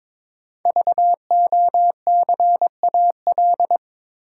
Morse code, heard as a keyed tone, about 22 wpm, 700 Hz. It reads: VOCAL
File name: 5 Letter Words - First 3 Letters Given - 22wpm